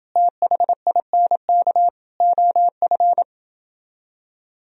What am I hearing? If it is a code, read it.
Morse code, heard as a keyed tone, about 30 wpm, 700 Hz. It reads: THINK OF